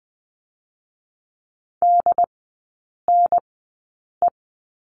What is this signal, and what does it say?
Morse code, heard as a keyed tone, about 20 wpm, spaced as for 8 wpm, 700 Hz. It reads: DNE
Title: 107 Lesson 18e - Letter m - Three Characters 20wpm